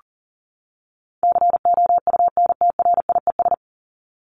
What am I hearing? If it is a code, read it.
Morse code, heard as a keyed tone, about 40 wpm, 700 Hz. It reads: COUNTRIES
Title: ES:S Suffix - Top 100 Words with Base Words 40wpm